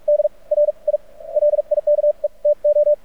Morse code, heard as a keyed tone, about 47 wpm, spaced as for 36 wpm, 590 Hz. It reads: DR I LIKE TO
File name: archived / CRSnet